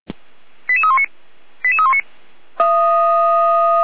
Five-tone selective call (ZVEI), heard as quick runs of five tones, tones of about 70 ms. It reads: 88319 88319
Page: Server at audio.f85360.feuerwehr-orlamuende.org